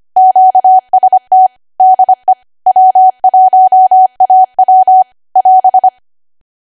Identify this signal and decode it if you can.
Morse code, heard as a keyed tone, about 25 wpm, 750 Hz. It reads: QST DE W1AW <AS>
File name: W1AW-Level3/140218 25WPM